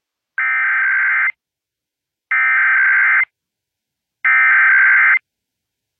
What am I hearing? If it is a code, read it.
Emergency Alert System (SAME) tones (header bursts), heard as harsh buzzing bursts. It is heard from 0.4 to 5.1 s.